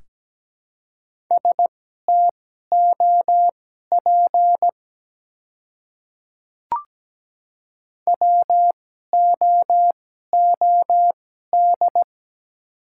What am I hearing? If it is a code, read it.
Morse code, heard as a keyed tone, about 17 wpm, spaced as for 12 wpm, 700 Hz. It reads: STOP WOOD
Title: 2x Character Spacing - Top 300 Words - 17wpm